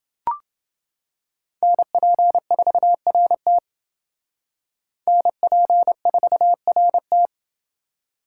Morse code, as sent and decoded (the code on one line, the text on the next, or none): -. .--. ....- .-. - / -. .--. ....- .-. -
NP4RT NP4RT